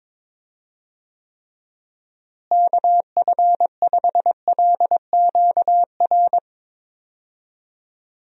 Morse code, heard as a keyed tone, about 21 wpm, 700 Hz. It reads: KF5LQR
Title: USA Call Signs - 6 Characters - Repeated 3x - 22wpm